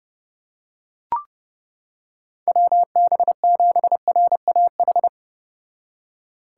Morse code, heard as a keyed tone, about 30 wpm, 700 Hz. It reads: WB7RAH